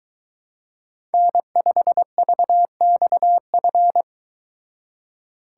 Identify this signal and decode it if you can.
Morse code, heard as a keyed tone, about 22 wpm, 700 Hz. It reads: N5VXF